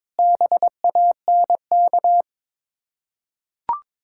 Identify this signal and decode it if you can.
Morse code, heard as a keyed tone, about 22 wpm, 690 Hz. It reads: BANK
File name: Difficult Words with Hints - 22wpm